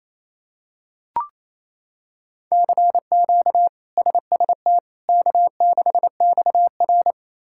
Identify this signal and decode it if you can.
Morse code, heard as a keyed tone, about 28 wpm, 700 Hz. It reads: CQ SST K6XR
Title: K1USN SST QSO Practice - 28wpm